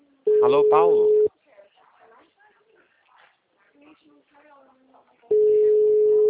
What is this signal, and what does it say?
Ringback tone (Europe, 425 Hz), with speech.